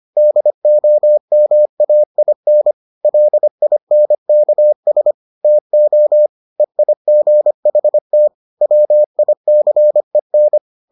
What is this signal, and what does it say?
Morse code, heard as a keyed tone, about 25 wpm, 600 Hz. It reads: DOMAIN LINKS TO EIGHT WICEN